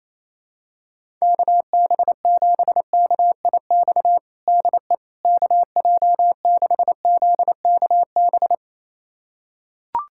Morse code, heard as a keyed tone, about 28 wpm, 700 Hz. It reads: KB7KIX DE KJ6ZKB
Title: USA Callsign DE Callsign 28wpm